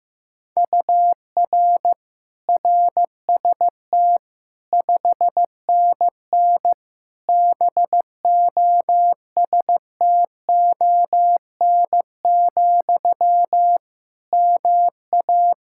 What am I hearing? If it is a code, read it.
Morse code, heard as a keyed tone, about 15 wpm, 700 Hz. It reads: UR RST 5NN BOSTON, MA
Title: Practice QSO - Exchange 1 - Beginning - 15wpm